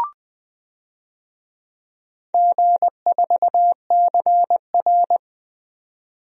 Morse code, heard as a keyed tone, about 20 wpm, 700 Hz. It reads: G4CR
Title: DXCC-100 Call Signs - 4 Characters - 20wpm